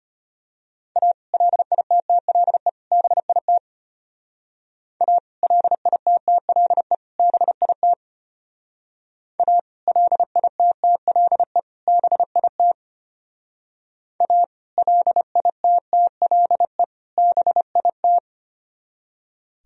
Morse code, heard as a keyed tone, about 31 wpm, 700 Hz. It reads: A LITTLE BIT A LITTLE BIT A LITTLE BIT A LITTLE BIT
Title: Speed-Racing - Sets of 3 Words 25wpm